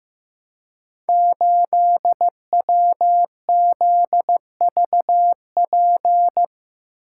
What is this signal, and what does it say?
Morse code, heard as a keyed tone, about 15 wpm, 700 Hz. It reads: 8WZVP
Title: Code Groups - 5 Letter:Numbers - 15wpm